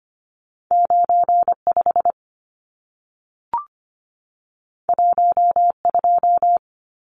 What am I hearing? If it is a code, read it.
Morse code, heard as a keyed tone, about 25 wpm, 700 Hz. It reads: 95 12
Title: Numbers - 2-Digit 25wpm